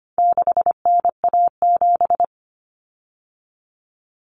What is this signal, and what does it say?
Morse code, heard as a keyed tone, about 25 wpm, 700 Hz. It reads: 6NA7
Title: Common Cut Numbers - 4-Digit 25wpm